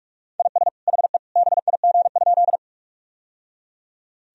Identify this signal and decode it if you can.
Morse code, heard as a keyed tone, about 42 wpm, 700 Hz. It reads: IS HE BIG?
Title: Sentences from Top 1000 Words - Review Part 1 - 45wpm